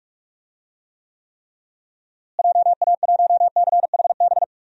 Morse code, heard as a keyed tone, about 45 wpm, 700 Hz. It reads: JA1CHB
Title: DXCC-100 Call Signs - 6 Characters - Repeated 3x - 45wpm